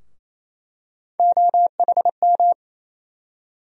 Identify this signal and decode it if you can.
Morse code, heard as a keyed tone, about 28 wpm, 700 Hz. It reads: OHM
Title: USA Call Sign Postfixes - 3 Letters 28wpm